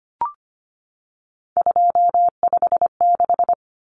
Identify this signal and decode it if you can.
Morse code, heard as a keyed tone, about 25 wpm, 700 Hz. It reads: 256
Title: Common Cut Numbers - 3-Digit 25wpm